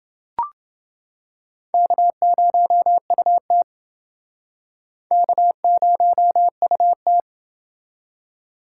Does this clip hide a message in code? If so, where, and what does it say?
Yes: Morse code from 1.7 to 7.2 s, heard as a keyed tone, about 29 wpm, 700 Hz. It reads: K0UT K0UT